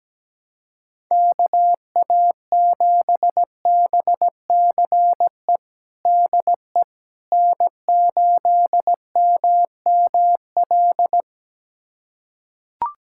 Morse code, heard as a keyed tone, about 17 wpm, 700 Hz. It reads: KA7BCE DE N8MML
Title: USA Callsign DE Callsign 17wpm